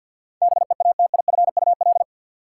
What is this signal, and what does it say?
Morse code, heard as a keyed tone, about 53 wpm, 700 Hz. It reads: BEATIFUL
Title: Sentences from Top 1300 Words - New Word Drill - 50wpm